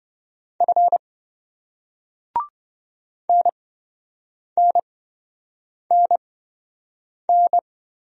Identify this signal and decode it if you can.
Morse code, heard as a keyed tone, about 26 wpm, spaced as for 7 wpm, 700 Hz. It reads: F NNNN